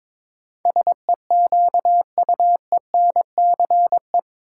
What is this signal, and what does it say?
Morse code, heard as a keyed tone, about 22 wpm, 700 Hz. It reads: SEQUENCE
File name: practice / 22wpm / Eight Letter Words - 22wpm